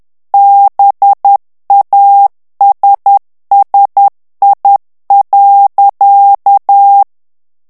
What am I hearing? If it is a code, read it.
Morse code, heard as a keyed tone, about 10 wpm, 800 Hz. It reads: BASSI.
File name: ITALIEN texte clair 2 vitesse 10 mots minute